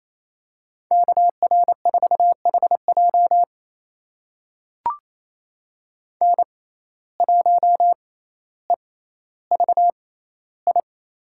Morse code, heard as a keyed tone, about 28 wpm, spaced as for 14 wpm, 700 Hz. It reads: KR4HJ N1EVI
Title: USA Call Signs - 5 Characters - 6x Character Spacing - 28wpm